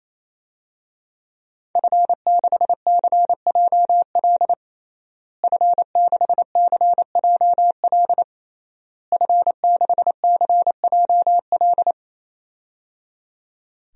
Morse code, heard as a keyed tone, about 27 wpm, 700 Hz. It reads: F6CJL F6CJL F6CJL
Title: DXCC-100 Call Signs - 5 Characters - Repeated 3x - 28wpm